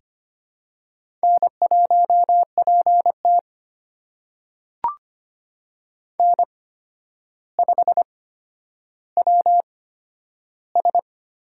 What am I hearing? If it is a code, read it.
Morse code, heard as a keyed tone, about 25 wpm, spaced as for 11 wpm, 700 Hz. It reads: N1PT N5WS
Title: USA Call Signs - 4 Characters - 8x Character Spacing - 25wpm